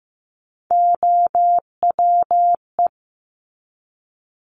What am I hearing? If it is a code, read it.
Morse code, heard as a keyed tone, about 15 wpm, 700 Hz. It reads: OWE
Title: Top 1501-1750 Words 15wpm